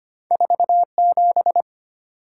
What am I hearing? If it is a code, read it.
Morse code, heard as a keyed tone, about 25 wpm, 700 Hz. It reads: 47
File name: Two Letter-Number Combinations - 25wpm